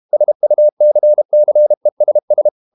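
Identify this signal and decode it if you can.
Morse code, heard as a keyed tone, about 32 wpm, 600 Hz. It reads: SUCCESS
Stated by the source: Morse code practice at 32 wpm from CW med Gnister, reading Ken Follett's novel Never in English.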